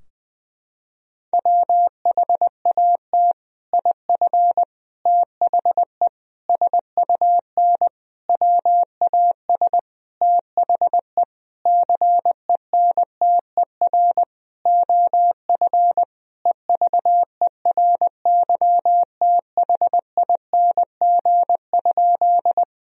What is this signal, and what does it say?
Morse code, heard as a keyed tone, about 20 wpm, 700 Hz. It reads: WHAT IF THE SUN WAS THE CENTER OF EVERYTHING?
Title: Sentences from Top 1000 Words - Review Part 2 - 20wpm